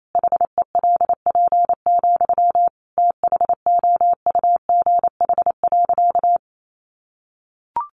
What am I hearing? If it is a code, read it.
Morse code, heard as a keyed tone, about 28 wpm, 700 Hz. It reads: HELP, THOUGH.
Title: Sentences from Top 1100 Words 28wpm